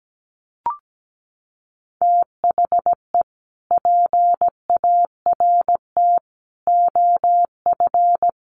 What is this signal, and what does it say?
Morse code, heard as a keyed tone, about 17 wpm, 700 Hz. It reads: THE PART OF
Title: Sets of 4 Words 17wpm